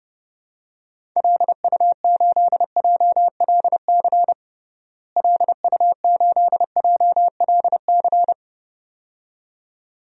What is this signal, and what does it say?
Morse code, heard as a keyed tone, about 30 wpm, 700 Hz. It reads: LU8JLC LU8JLC
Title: DXCC-100 Call Signs - 6 Characters - Repeated 2x - 30wpm